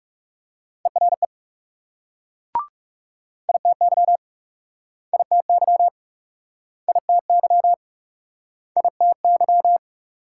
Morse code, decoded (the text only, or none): ERE ITY ITY ITY ITY